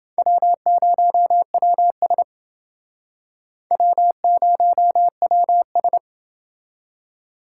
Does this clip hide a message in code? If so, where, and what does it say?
Yes: Morse code from 0.2 to 6.0 s, heard as a keyed tone, about 28 wpm, 700 Hz. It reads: W0WS W0WS